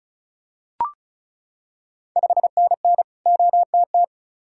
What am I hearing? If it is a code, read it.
Morse code, heard as a keyed tone, about 34 wpm, 690 Hz. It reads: 5NN OTT
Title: Ontario QSO Party Exchange 35wpm